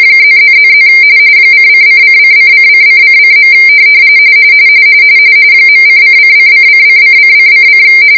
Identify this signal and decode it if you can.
RTTY, about 45.45 baud. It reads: ++++ :::::::::::. .::::::::::::